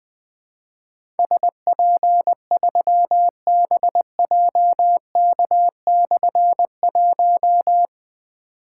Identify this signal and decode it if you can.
Morse code, heard as a keyed tone, about 20 wpm, 700 Hz. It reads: SP3BJK/1
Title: Difficult Call Signs 20wpm